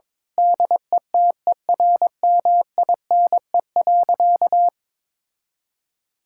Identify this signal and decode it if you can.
Morse code, heard as a keyed tone, about 22 wpm, 700 Hz. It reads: DETERMINE.
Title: Sentences from Top 1000 Words - Review Part 5 - 22wpm